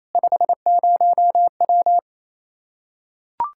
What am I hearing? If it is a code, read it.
Morse code, heard as a keyed tone, about 28 wpm, 700 Hz. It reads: 50W
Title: K1USN SST QSO Practice 28wpm